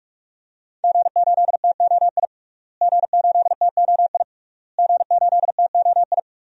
Morse code, heard as a keyed tone, about 46 wpm, 700 Hz. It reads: G8TOI G8TOI G8TOI
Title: DXCC-100 Call Signs - 5 Characters - Repeated 3x - 45wpm